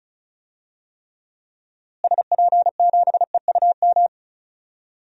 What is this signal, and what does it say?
Morse code, heard as a keyed tone, about 34 wpm, 700 Hz. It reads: SP7EUM
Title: DXCC-100 Call Signs - 6 Characters - Repeated 2x - 35wpm